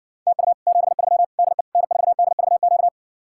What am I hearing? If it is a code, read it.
Morse code, heard as a keyed tone, about 60 wpm, 700 Hz. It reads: TU 73 DE N4DVQ